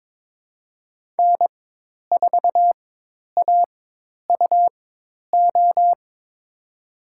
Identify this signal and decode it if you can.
Morse code, heard as a keyed tone, about 22 wpm, spaced as for 10 wpm, 700 Hz. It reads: N4AUO
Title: USA Call Signs - 4x Character Spacing - 22wpm